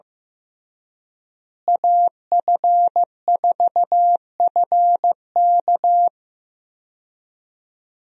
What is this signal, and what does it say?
Morse code, heard as a keyed tone, about 15 wpm, 700 Hz. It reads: AF4FK
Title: USA Call Signs - 5 Characters - Repeated 3x - 15wpm